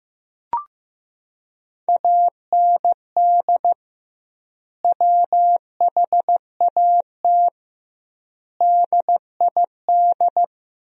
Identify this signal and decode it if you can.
Morse code, heard as a keyed tone, about 15 wpm, 700 Hz. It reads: AND WHAT DID
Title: 2x Word-Spacing - Sentences from Top 100 Words - 15wpm